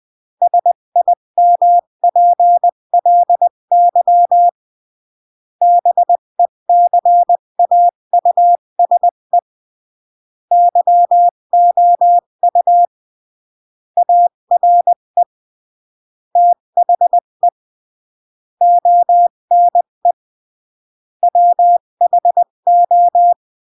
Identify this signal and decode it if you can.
Morse code, heard as a keyed tone, about 20 wpm, 700 Hz. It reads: SIMPLY BECAUSE YOU ARE THE ONE WHO
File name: CW / jodorowsky / 82 commandments 20-15